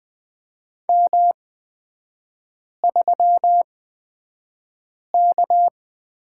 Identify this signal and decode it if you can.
Morse code, heard as a keyed tone, about 20 wpm, spaced as for 5 wpm, 700 Hz. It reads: M3K
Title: 224 Lesson 36e - BT - Three Characters 20wpm